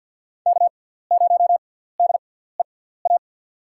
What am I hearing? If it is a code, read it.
Morse code, heard as a keyed tone, about 45 wpm, spaced as for 17 wpm, 700 Hz. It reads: K0DEA
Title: USA Call Signs - 5 Characters - 6x Character Spacing - 50wpm